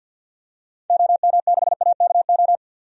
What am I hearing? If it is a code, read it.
Morse code, heard as a keyed tone, about 52 wpm, 700 Hz. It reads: OM6AKO
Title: DXCC-100 Call Signs - 6 Characters - 50wpm